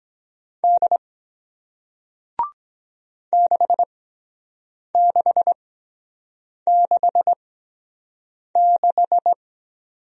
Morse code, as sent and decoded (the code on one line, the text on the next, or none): -.. / -.... -.... -.... -....
D 6666